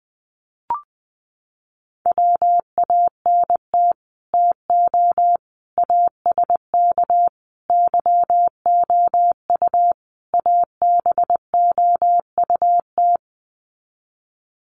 Morse code, heard as a keyed tone, about 20 wpm, 700 Hz. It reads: WANT TO ASK YOU ABOUT
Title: Sets of 5 Words Encore2 20wpm